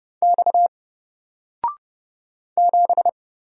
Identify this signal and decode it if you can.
Morse code, heard as a keyed tone, about 31 wpm, 700 Hz. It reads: X 7
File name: Single-Letter-Number-30wpm